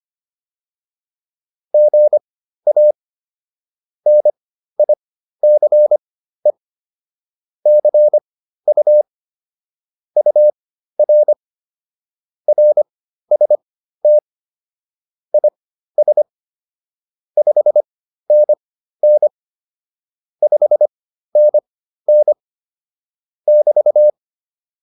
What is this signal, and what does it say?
Morse code, heard as a keyed tone, about 25 wpm, spaced as for 13 wpm, 600 Hz. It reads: GA NICE CU UR RST IS 5NN 5NN <BT>